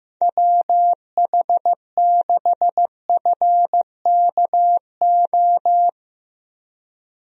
Morse code, heard as a keyed tone, about 16 wpm, 700 Hz. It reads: WH6FKO
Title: USA Call Signs - 6 Characters - 15wpm